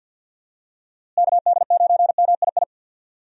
Morse code, heard as a keyed tone, about 53 wpm, 700 Hz. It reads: KD9MII
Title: USA Call Signs - 6 Characters - Repeated 2x - 50wpm